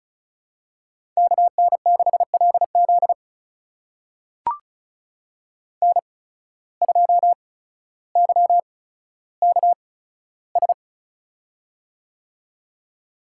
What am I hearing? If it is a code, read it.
Morse code, heard as a keyed tone, about 35 wpm, spaced as for 15 wpm, 700 Hz. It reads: KN6LZ N2YKS
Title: USA Call Signs - 5 Characters - 8x Character Spacing - 35wpm